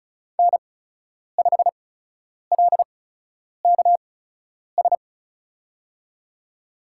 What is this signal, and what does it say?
Morse code, heard as a keyed tone, about 36 wpm, spaced as for 10 wpm, 700 Hz. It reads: N5LKS
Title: USA Call Signs - 5 Characters - 8x Character Spacing - 35wpm